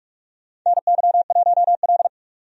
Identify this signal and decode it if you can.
Morse code, heard as a keyed tone, about 46 wpm, 700 Hz. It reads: NY1L
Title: USA Call Signs - 4 Characters - 45wpm